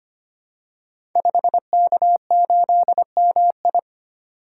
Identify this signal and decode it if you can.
Morse code, heard as a keyed tone, about 25 wpm, 700 Hz. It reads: 5K8MI
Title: Code Groups - 5 Letter:Numbers - 25wpm